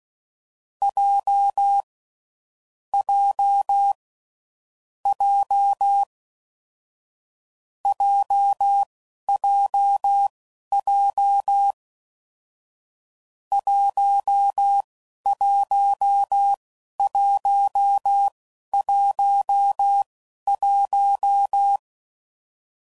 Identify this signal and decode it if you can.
Morse code, heard as a keyed tone, about 16 wpm, spaced as for 11 wpm, 800 Hz. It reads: J J J JJJ 11111
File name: Combinations / Difficult Combos / AWJ1